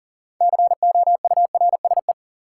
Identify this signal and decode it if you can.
Morse code, heard as a keyed tone, about 38 wpm, 700 Hz. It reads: COURSE
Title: Sentences from Top 500 Words - New Word Drill - 40wpm